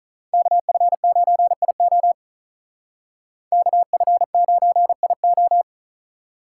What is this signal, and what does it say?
Morse code, heard as a keyed tone, about 45 wpm, 700 Hz. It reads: KF9IO KF9IO